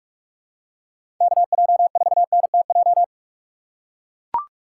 Morse code, heard as a keyed tone, about 43 wpm, 700 Hz. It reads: KJ4NTJ
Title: USA Call Signs - 8x Character Spacing - 45wpm